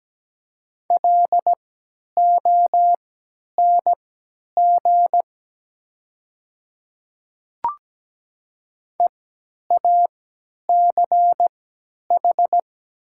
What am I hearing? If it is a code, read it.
Morse code, heard as a keyed tone, about 17 wpm, spaced as for 10 wpm, 700 Hz. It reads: LONG EACH
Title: 3x Character Spacing - Top 100 Words - 17wpm